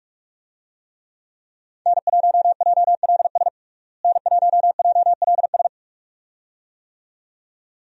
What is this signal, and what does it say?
Morse code, heard as a keyed tone, about 45 wpm, 700 Hz. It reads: N1JLS N1JLS
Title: USA Call Signs - 5 Characters - Repeated 2x - 45wpm